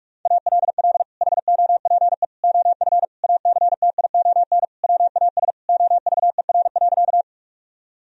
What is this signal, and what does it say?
Morse code, heard as a keyed tone, about 46 wpm, 700 Hz. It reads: ALL HOPE OF ACTION WAS OVER.
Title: Sentences from Top 1000 Words - Review Part 4 - 45wpm